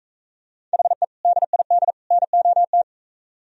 Morse code, heard as a keyed tone, about 42 wpm, 700 Hz. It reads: HE DID NOT